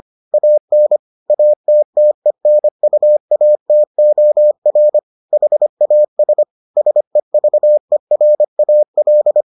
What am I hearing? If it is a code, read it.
Morse code, heard as a keyed tone, about 25 wpm, 600 Hz. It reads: AN ATTENUATOR HAS SEVERAL